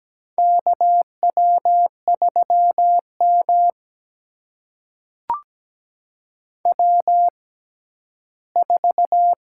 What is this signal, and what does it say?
Morse code, heard as a keyed tone, about 17 wpm, spaced as for 12 wpm, 700 Hz. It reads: KW3M W4
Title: USA Call Signs - 4 Characters - 6x Character Spacing - 17wpm